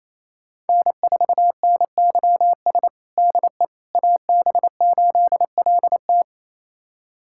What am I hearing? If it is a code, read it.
Morse code, heard as a keyed tone, about 28 wpm, 700 Hz. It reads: N4NYS DE AB8LT
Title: USA Callsign DE Callsign - 5 Characters - 28wpm